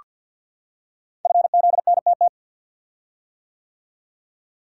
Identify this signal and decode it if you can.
Morse code, heard as a keyed tone, about 48 wpm, 700 Hz. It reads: FZNTT
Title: Code Groups - 5 Letters - 50wpm